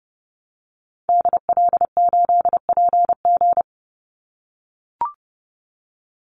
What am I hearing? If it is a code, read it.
Morse code, heard as a keyed tone, about 30 wpm, 700 Hz. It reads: DL8PG
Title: Call Signs - 30wpm